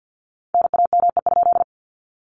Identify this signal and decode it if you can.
Morse code, heard as a keyed tone, about 48 wpm, 700 Hz. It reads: NAME?
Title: Common QSO Phrases Encore 50wpm